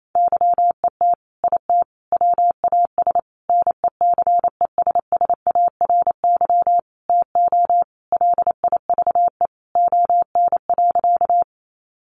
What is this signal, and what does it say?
Morse code, heard as a keyed tone, about 28 wpm, 700 Hz. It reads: YET IT WAS NECESSARY TO LIVE ON.
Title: Sentences from Top 800 Words 28wpm